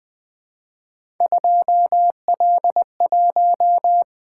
Morse code, heard as a keyed tone, about 20 wpm, 700 Hz. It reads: 2L1
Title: Three Letter-Number Combinations - 20wpm